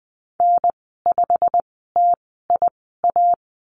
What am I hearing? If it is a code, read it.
Morse code, heard as a keyed tone, about 20 wpm, spaced as for 15 wpm, 700 Hz. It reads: N5TIA
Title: 088 Lesson 15g - Number 5 - Call Signs 20wpm